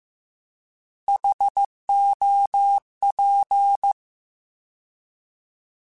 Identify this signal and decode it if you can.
Morse code, heard as a keyed tone, about 15 wpm, 800 Hz. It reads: HOP
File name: Alphabetical / Alphabetical mp3s / Hs @13